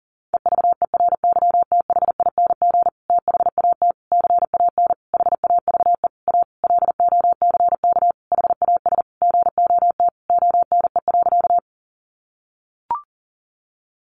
Morse code, heard as a keyed tone, about 40 wpm, 700 Hz. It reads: EVERYTHING THAT CAN HAVE A LOCK HAS GOT ONE.
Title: Sentences from Top 1300 Words 40wpm